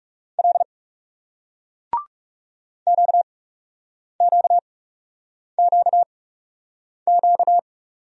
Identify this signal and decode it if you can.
Morse code, heard as a keyed tone, about 40 wpm, spaced as for 8 wpm, 700 Hz. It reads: L QQQQ